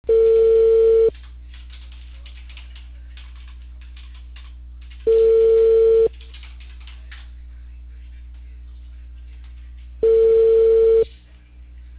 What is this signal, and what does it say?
Ringback tone (China, 450 Hz).